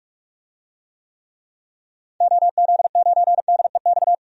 Morse code, heard as a keyed tone, about 46 wpm, 700 Hz. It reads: OZ9DEX
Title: DXCC-100 Call Signs - 6 Characters - 45wpm